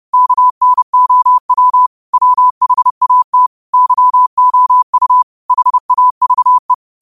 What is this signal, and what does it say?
Morse code, heard as a keyed tone, about 30 wpm, 1000 Hz. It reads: KNOW WHAT YOU HAVE
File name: qotdfiles / 30wpm